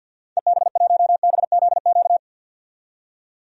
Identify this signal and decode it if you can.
Morse code, heard as a keyed tone, about 51 wpm, 700 Hz. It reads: EB1BZQ
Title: DXCC-100 Call Signs - 6 Characters - 50wpm